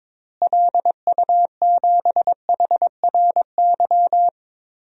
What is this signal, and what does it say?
Morse code, heard as a keyed tone, about 22 wpm, 700 Hz. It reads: LU7HRY
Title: DXCC-100 Call Signs - 6 Characters - Repeated 3x - 22wpm